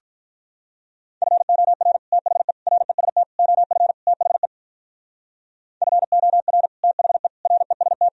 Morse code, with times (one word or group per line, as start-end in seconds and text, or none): 1.2-2.0 FOR
2.1-2.5 THE
2.7-3.2 REST
3.4-3.9 OF
4.1-4.5 THE
5.8-6.7 FOR
6.8-7.3 THE
7.5-8.1 REST